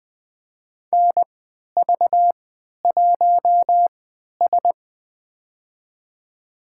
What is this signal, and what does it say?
Morse code, heard as a keyed tone, about 20 wpm, spaced as for 11 wpm, 700 Hz. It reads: NV1S
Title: USA Call Signs - 4 Characters - 3x Character Spacing - 20wpm